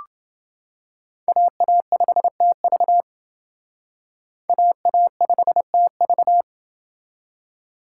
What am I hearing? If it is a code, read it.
Morse code, heard as a keyed tone, about 28 wpm, 700 Hz. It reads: AA5TV AA5TV